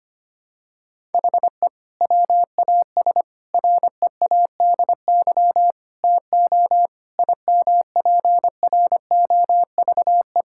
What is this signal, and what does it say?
Morse code, heard as a keyed tone, about 25 wpm, 700 Hz. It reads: HE WAS READY TO IMPROVE
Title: Sentences from Top 1000 Words - Review Part 5 - 25wpm